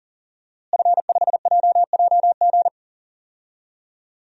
Morse code, heard as a keyed tone, about 40 wpm, 700 Hz. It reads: F5JJG
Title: DXCC-100 Call Signs - 5 Characters - Repeated 2x - 40wpm